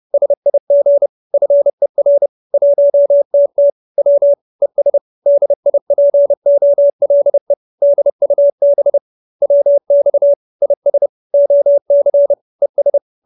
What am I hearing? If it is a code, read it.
Morse code, heard as a keyed tone, about 30 wpm, 570 Hz. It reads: SIG FER 1TT W ES DIPOLE DUB WX IS OC ES
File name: QSO314 30